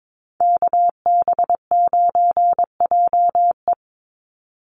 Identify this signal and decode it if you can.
Morse code, heard as a keyed tone, about 22 wpm, 700 Hz. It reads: KB9JE
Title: USA Call Signs 22wpm